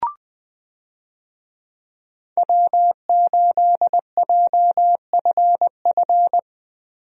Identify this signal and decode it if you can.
Morse code, heard as a keyed tone, about 21 wpm, 700 Hz. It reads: W8JFF
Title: USA Call Signs - 5 Characters - Repeated 3x - 20wpm